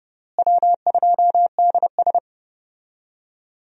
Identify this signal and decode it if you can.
Morse code, heard as a keyed tone, about 30 wpm, 700 Hz. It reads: W2DS
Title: USA Call Signs - 4 Characters - Repeated 3x - 30wpm